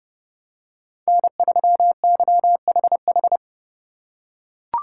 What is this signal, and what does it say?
Morse code, heard as a keyed tone, about 30 wpm, 700 Hz. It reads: N3YHH